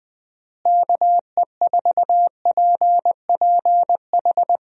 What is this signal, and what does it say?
Morse code, heard as a keyed tone, about 20 wpm, 700 Hz. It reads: KE4PPH
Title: USA Call Signs - 6 Characters - Repeated 3x - 20wpm